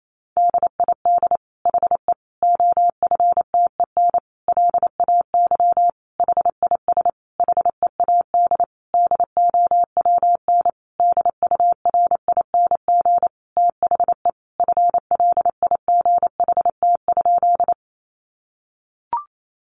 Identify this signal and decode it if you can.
Morse code, heard as a keyed tone, about 28 wpm, 700 Hz. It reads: DID HE OFTEN LAY HIS HEAD DOWN DURING THE FLIGHT?
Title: Sentences from Top 1000 Words 28wpm